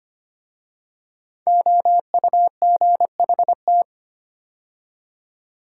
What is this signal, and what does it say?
Morse code, heard as a keyed tone, about 25 wpm, 700 Hz. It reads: OUGHT
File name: 5 Letter Words - First 3 Letters Given - 25wpm